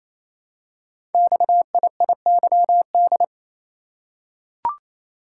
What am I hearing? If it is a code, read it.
Morse code, heard as a keyed tone, about 27 wpm, 700 Hz. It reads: XIIYD
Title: Code Groups - 5 Letter:Numbers - 28wpm